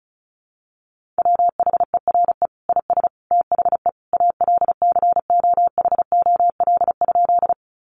Morse code, heard as a keyed tone, about 34 wpm, 700 Hz. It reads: WHERE IS THE ALCOHOL?
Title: Sentences from Top 1400 Words 35wpm